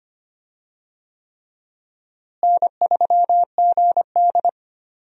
Morse code, heard as a keyed tone, about 25 wpm, 700 Hz. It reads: N3GD
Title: USA Call Signs - 4 Characters - Repeated 3x - 25wpm